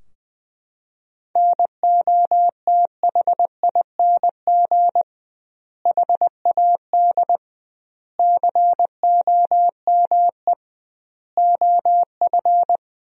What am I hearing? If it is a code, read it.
Morse code, heard as a keyed tone, about 20 wpm, 700 Hz. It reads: NOTHING HAD COME OF
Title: 2x Word-Spacing - Sentences from Top 500 Words - 20wpm